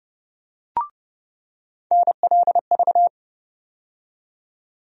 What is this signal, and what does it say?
Morse code, heard as a keyed tone, about 30 wpm, 700 Hz. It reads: NLV
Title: USA Call Sign Postfixes - 3 Letters 30wpm